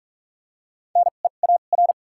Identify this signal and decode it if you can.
Morse code, heard as a keyed tone, about 46 wpm, spaced as for 33 wpm, 700 Hz. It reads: NEAR
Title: 2x Character Spacing - Top 200 Words - 45wpm